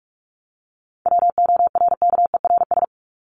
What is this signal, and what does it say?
Morse code, heard as a keyed tone, about 44 wpm, 700 Hz. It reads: WORKERS
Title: ES:S Suffix - Top 100 Words with Base Words 45wpm